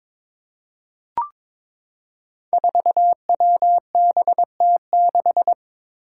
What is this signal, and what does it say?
Morse code, heard as a keyed tone, about 22 wpm, 700 Hz. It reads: 4WBT6
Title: Five Challenging Letter-Number Combinations 22wpm